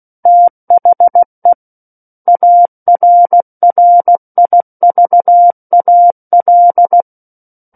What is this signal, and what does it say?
Morse code, heard as a keyed tone, about 16 wpm, 700 Hz. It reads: THE ARRIVAL
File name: Book One / 07-Chapter 7 - 16 WPM